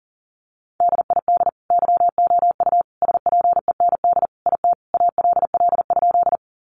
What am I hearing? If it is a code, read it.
Morse code, heard as a keyed tone, about 40 wpm, 700 Hz. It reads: DID YOU SPEND IT ALL?
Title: Sentences from Top 800 Words 40wpm